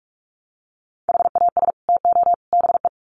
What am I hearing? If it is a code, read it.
Morse code, heard as a keyed tone, about 44 wpm, 700 Hz. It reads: HAS TO BE
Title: Sets of 3 Words 45wpm